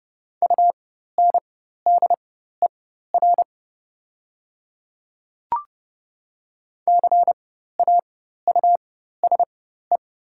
Morse code, heard as a keyed tone, about 29 wpm, spaced as for 14 wpm, 700 Hz. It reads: UNDER CAUSE
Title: 4x Character Spacing - Top 200 Words - 30wpm